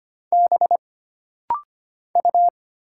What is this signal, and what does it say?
Morse code, heard as a keyed tone, about 25 wpm, 700 Hz. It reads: B U
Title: Single Letter-Number Alt - Rapid-Fire 25wpm